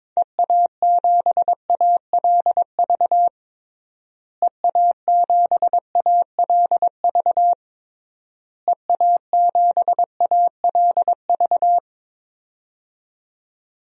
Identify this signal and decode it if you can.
Morse code, heard as a keyed tone, about 22 wpm, 700 Hz. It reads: EA7ALV EA7ALV EA7ALV
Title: DXCC-100 Call Signs - 6 Characters - Repeated 3x - 22wpm